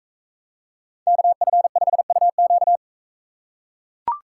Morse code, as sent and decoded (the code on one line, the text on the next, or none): -.- ..-. ..... ..- --.-
KF5UQ